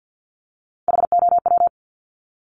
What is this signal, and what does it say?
Morse code, heard as a keyed tone, about 50 wpm, 700 Hz. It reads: HOW
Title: Multi-letter Phonemes Repeated 3x 50wpm